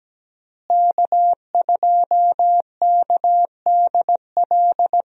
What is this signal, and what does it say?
Morse code, heard as a keyed tone, about 17 wpm, 700 Hz. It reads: K2KDL
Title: USA Call Signs - 5 Characters - Repeated 2x - 17wpm